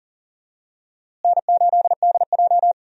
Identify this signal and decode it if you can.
Morse code, heard as a keyed tone, about 40 wpm, 700 Hz. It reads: N8DJ
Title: USA Call Signs - 4 Characters - Repeated 2x - 40wpm